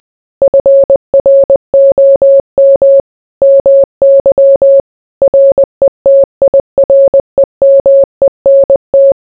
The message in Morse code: FROM MY RETIREMENT

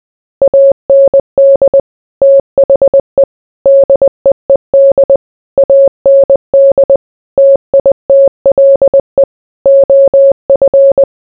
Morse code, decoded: AND THE DEED AND TITLE OF

